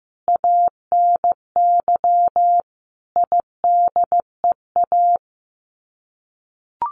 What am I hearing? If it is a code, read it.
Morse code, heard as a keyed tone, about 15 wpm, 700 Hz. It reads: ANY IDEA
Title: Sets of 5 Words Encore 15wpm